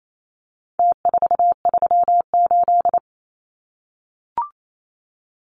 Morse code, heard as a keyed tone, about 28 wpm, 700 Hz. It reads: T438
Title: Numbers - Contest Style - 4-Digit 28wpm